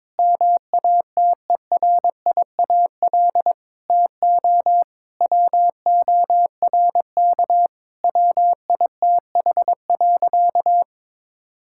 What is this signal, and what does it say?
Morse code, heard as a keyed tone, about 22 wpm, 700 Hz. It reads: MATERIAL TO WORK WITH.
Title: Sentences from Top 1000 Words - Review Part 5 - 22wpm